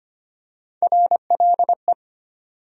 Morse code, heard as a keyed tone, about 25 wpm, 700 Hz. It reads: RLE
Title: USA Call Sign Postfixes - 3 Letters 25wpm